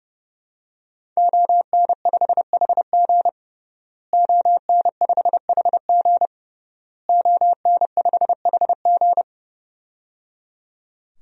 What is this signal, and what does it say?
Morse code, heard as a keyed tone, about 30 wpm, 700 Hz. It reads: ON5HG ON5HG ON5HG
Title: DXCC-100 Call Signs - 5 Characters - Repeated 3x - 30wpm